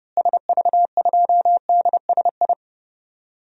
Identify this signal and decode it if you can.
Morse code, heard as a keyed tone, about 30 wpm, 700 Hz. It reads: SV2DSI
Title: DXCC-100 Call Signs - 6 Characters - Repeated 3x - 30wpm